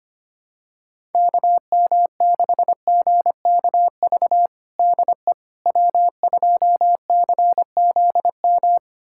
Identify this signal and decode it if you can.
Morse code, heard as a keyed tone, about 25 wpm, 700 Hz. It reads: KM6GKV DE W2CZM
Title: USA Callsign DE Callsign 25wpm